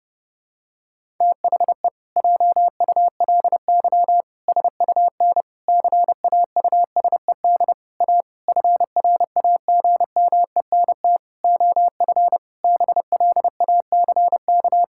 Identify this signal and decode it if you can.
Morse code, heard as a keyed tone, about 30 wpm, 700 Hz. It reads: THE JULY SUN CAUSED A FRAGMENT OF BLACK
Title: Pangram Challenge v2 - 30wpm